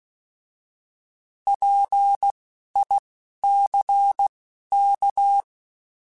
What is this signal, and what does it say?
Morse code, heard as a keyed tone, about 16 wpm, spaced as for 11 wpm, 800 Hz. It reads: PICK
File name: Alphabetical / Alphabetical mp3s / Ps @10